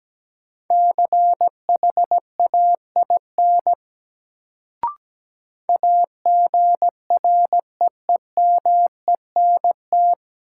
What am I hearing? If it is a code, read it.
Morse code, heard as a keyed tone, about 17 wpm, 700 Hz. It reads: CHAIN AGREEMENT
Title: Sentences from Top 1500 Words - New Word Drill - 17wpm